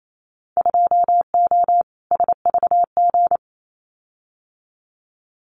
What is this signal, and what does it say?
Morse code, heard as a keyed tone, about 28 wpm, 700 Hz. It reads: 2O SVG
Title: Winter Field Day Exchanges 28wpm